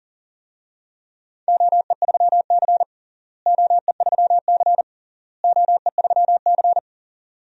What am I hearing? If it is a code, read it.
Morse code, heard as a keyed tone, about 40 wpm, 700 Hz. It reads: OE3C OE3C OE3C
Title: DXCC-100 Call Signs - 4 Characters - Repeated 3x - 40wpm